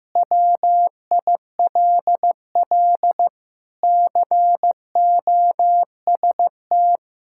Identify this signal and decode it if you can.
Morse code, heard as a keyed tone, about 15 wpm, 700 Hz. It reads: WILL COST
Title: Sentences from Top 1000 Words - Review Part 5 - 15wpm